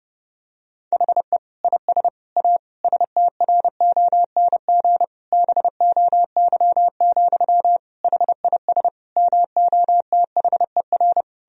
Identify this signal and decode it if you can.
Morse code, heard as a keyed tone, about 30 wpm, 700 Hz. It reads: HE IS A STRONG BOY, HIS MOTHER
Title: Sentences from Top 1000 Words - Review Part 3 - 30wpm